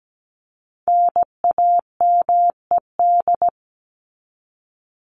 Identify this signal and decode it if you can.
Morse code, heard as a keyed tone, about 17 wpm, 700 Hz. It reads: NAMED
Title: ED Suffix - Top 100 Words with Base Words 17wpm